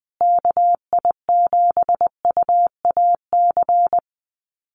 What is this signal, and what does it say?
Morse code, heard as a keyed tone, about 20 wpm, 700 Hz. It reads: KI7UAC
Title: USA Call Signs Repeated 3x 20wpm